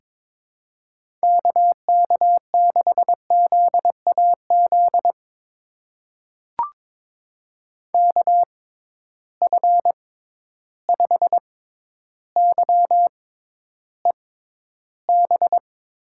Morse code, heard as a keyed tone, about 22 wpm, spaced as for 11 wpm, 700 Hz. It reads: KK6ZAZ KF5YEB
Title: USA Call Signs - 6 Characters - 6x Character Spacing - 22wpm